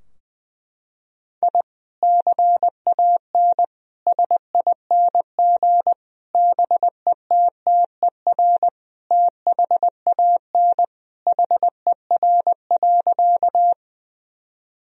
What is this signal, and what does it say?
Morse code, heard as a keyed tone, about 20 wpm, 700 Hz. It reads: I CAN SING BETTER THAN HER.
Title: Sentences from Top 1000 Words - Review Part 2 - 20wpm